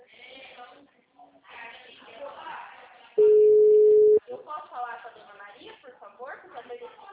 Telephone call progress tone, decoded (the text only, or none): ringback tone